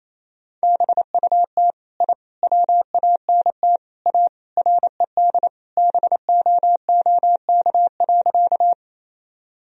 Morse code, heard as a keyed tone, about 28 wpm, 700 Hz. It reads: BUT I WANT A RED BOOK.
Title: Sentences from Top 1000 Words - Review Part 2 - 28wpm